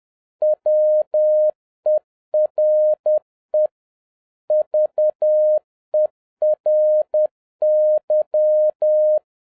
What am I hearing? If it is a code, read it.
Morse code, heard as a keyed tone, about 10 wpm, 610 Hz. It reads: WERE VERY
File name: 2011-07-mp3 / n56 BACK FROM A BIG ACCIDENT 10wpm